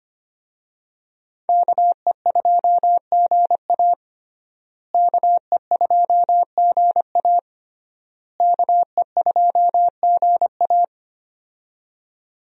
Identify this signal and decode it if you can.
Morse code, heard as a keyed tone, about 25 wpm, 700 Hz. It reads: KE2GA KE2GA KE2GA
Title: USA Call Signs - 5 Characters - Repeated 3x - 25wpm